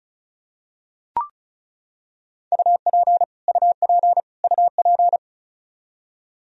Morse code, heard as a keyed tone, about 34 wpm, 700 Hz. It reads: UP UP UP
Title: Top 100 Words Repeated 35wpm